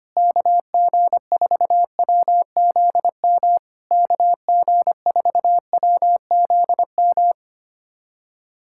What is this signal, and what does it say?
Morse code, heard as a keyed tone, about 25 wpm, 700 Hz. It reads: KG4WZM KG4WZM
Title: SOTA QSO Practice 25wpm